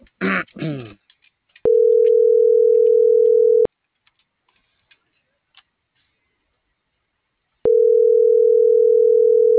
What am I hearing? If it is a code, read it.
Ringback tone (North America), 440 + 480 Hz.